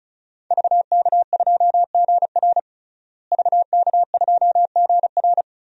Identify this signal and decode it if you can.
Morse code, heard as a keyed tone, about 36 wpm, 700 Hz. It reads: VK2GR VK2GR
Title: DXCC-100 Call Signs - 5 Characters - Repeated 3x - 35wpm